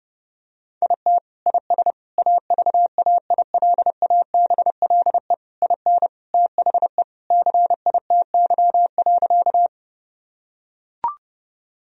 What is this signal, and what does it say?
Morse code, heard as a keyed tone, about 30 wpm, 700 Hz. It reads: IT IS AVAILABLE IN THE CITY.
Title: Sentences from Top 1000 Words - Review Part 3 - 30wpm